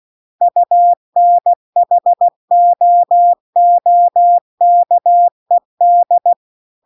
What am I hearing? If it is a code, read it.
Morse code, heard as a keyed tone, about 16 wpm, 700 Hz. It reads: UNHOOKED